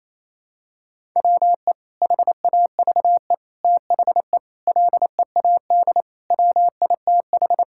Morse code, heard as a keyed tone, about 28 wpm, 700 Hz. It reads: WE HAVE THE LEAD WITH
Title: Sentences from Top 1000 Words - Review Part 5 - 28wpm